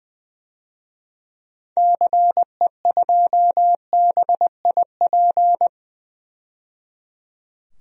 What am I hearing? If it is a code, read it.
Morse code, heard as a keyed tone, about 20 wpm, 700 Hz. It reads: CE2BIP
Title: DXCC-100 Call Signs - 6 Characters - 20wpm